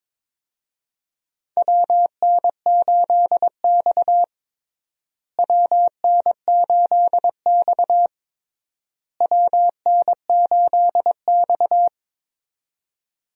Morse code, heard as a keyed tone, about 21 wpm, 700 Hz. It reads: WN8X WN8X WN8X
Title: USA Call Signs - 4 Characters - Repeated 3x - 22wpm